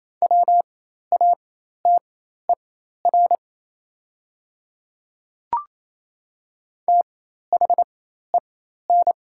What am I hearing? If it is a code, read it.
Morse code, heard as a keyed tone, about 28 wpm, spaced as for 13 wpm, 690 Hz. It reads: WATER THEN
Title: 4x Character Spacing - Top 100 Words - 28wpm